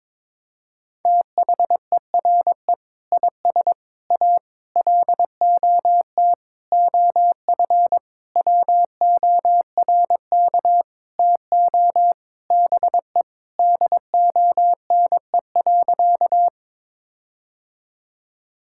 Morse code, heard as a keyed tone, about 22 wpm, 700 Hz. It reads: THERE IS A LOT OF WORK TO BE DONE.
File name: Sentences with 7-Word Phrases - 22wpm